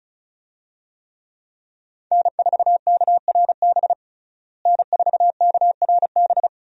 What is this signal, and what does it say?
Morse code, heard as a keyed tone, about 36 wpm, 700 Hz. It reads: N4KRB N4KRB
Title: USA Call Signs - 5 Characters - Repeated 2x - 35wpm